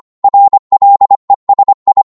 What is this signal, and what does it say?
Morse code, heard as a keyed tone, about 25 wpm, 800 Hz. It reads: RLESI
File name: Morse / ltr-25wpm